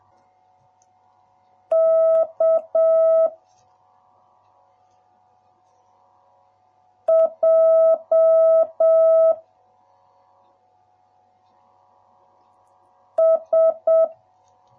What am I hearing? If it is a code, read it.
Morse code, heard as a keyed tone, about 7 wpm, 650 Hz. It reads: K J S